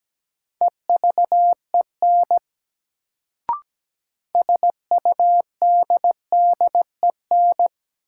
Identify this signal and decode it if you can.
Morse code, heard as a keyed tone, about 17 wpm, 700 Hz. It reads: EVEN SUDDEN
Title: Sets of 4 Words - New Word Drill - 17wpm